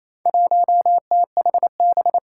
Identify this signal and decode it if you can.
Morse code, heard as a keyed tone, about 28 wpm, 700 Hz. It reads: 1THB
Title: Four Challenging Letter-Number Combinations 28wpm